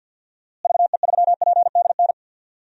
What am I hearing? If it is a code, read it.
Morse code, heard as a keyed tone, about 52 wpm, 700 Hz. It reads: VE3PDN
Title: DXCC-100 Call Signs - 6 Characters - Repeated 3x - 50wpm